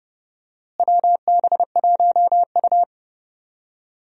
Morse code, heard as a keyed tone, about 30 wpm, 700 Hz. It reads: WB1U
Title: USA Call Signs - 4 Characters - Repeated 3x - 30wpm